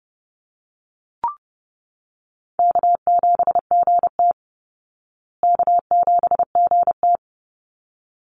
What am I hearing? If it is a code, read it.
Morse code, heard as a keyed tone, about 30 wpm, 700 Hz. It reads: K7GT K7GT
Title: Easy Call Signs Repeated 2x - 30wpm